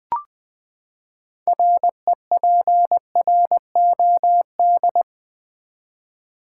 Morse code, heard as a keyed tone, about 20 wpm, 700 Hz. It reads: REPROD
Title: Build A Word - 6 Letters At A Time - 12-Letter Words - 20wpm